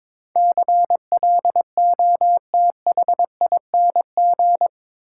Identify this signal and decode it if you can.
Morse code, heard as a keyed tone, about 22 wpm, 700 Hz. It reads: CLOTHING
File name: practice / 22wpm / Eight Letter Words - 22wpm